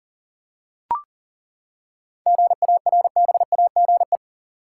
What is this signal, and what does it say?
Morse code, heard as a keyed tone, about 40 wpm, 700 Hz. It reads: GARBAGE
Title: Seven Letter Words - 40wpm